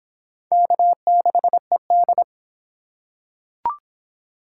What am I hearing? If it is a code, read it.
Morse code, heard as a keyed tone, about 25 wpm, 700 Hz. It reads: K6ED